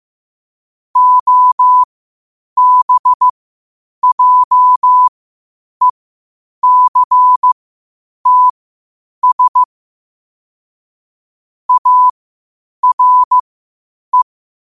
Morse code, heard as a keyed tone, about 15 wpm, spaced as for 8 wpm, 1000 Hz. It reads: OBJECTS ARE
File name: qotdfiles / 8wpm